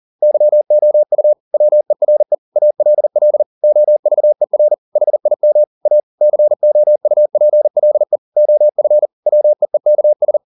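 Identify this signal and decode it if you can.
Morse code, heard as a keyed tone, about 40 wpm, 600 Hz. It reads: YOU WERE ALL OVER HIM A COUPLE OF WEEKS
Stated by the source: Morse code at 40 wpm, practice text, CW med Gnister.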